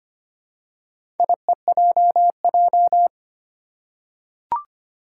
Morse code, heard as a keyed tone, about 25 wpm, 700 Hz. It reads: IEJJ
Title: Four Challenging Letter-Number Combinations 25wpm